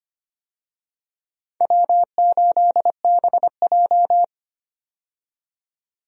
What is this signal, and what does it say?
Morse code, heard as a keyed tone, about 25 wpm, 700 Hz. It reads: W8BJ
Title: USA Call Signs - 4 Characters - 25wpm